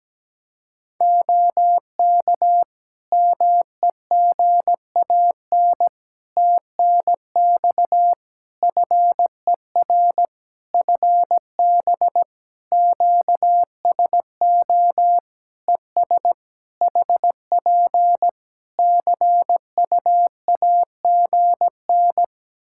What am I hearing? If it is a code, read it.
Morse code, heard as a keyed tone, about 17 wpm, 700 Hz. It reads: OK MEGAN TNX FER FB QSO ES HP CUAGN
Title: Practice QSO - Exchange 4 - Ending - 17wpm